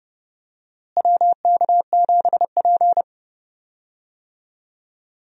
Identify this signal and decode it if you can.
Morse code, heard as a keyed tone, about 30 wpm, 700 Hz. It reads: WK7P
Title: USA Call Signs - 4 Characters - Repeated 2x - 30wpm